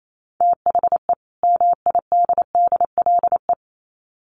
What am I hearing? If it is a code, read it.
Morse code, heard as a keyed tone, about 28 wpm, 700 Hz. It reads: THE MIDDLE
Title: Sets of 2 Words 28wpm